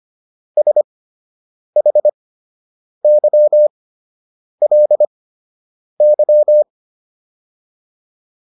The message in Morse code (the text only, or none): SHYLY